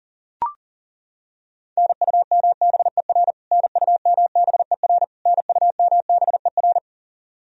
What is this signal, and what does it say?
Morse code, heard as a keyed tone, about 40 wpm, 700 Hz. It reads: NUMBER NUMBER NUMBER
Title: Top 100 Words Repeated 40wpm